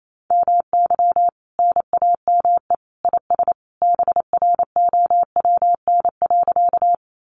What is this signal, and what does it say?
Morse code, heard as a keyed tone, about 28 wpm, 700 Hz. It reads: MY NAME IS BROWN.
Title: Sentences from Top 1200 Words 28wpm